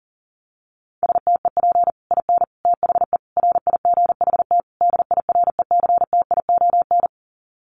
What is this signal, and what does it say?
Morse code, heard as a keyed tone, about 40 wpm, 700 Hz. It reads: STEP IN THE RIGHT DIRECTION
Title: Sets of 5 Words Encore 40wpm